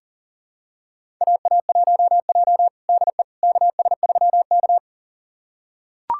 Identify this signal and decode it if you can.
Morse code, heard as a keyed tone, about 40 wpm, 700 Hz. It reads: AA1J DE KS3K
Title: USA Callsign DE Callsign - 4 Characters - 40wpm